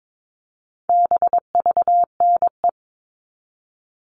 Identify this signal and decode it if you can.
Morse code, heard as a keyed tone, about 22 wpm, 700 Hz. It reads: BVNE
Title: All Cut Numbers - 4-Digit 22wpm